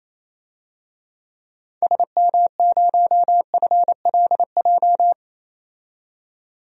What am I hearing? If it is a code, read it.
Morse code, heard as a keyed tone, about 28 wpm, 700 Hz. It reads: SM0FLJ
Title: DXCC-100 Call Signs - 6 Characters - 28wpm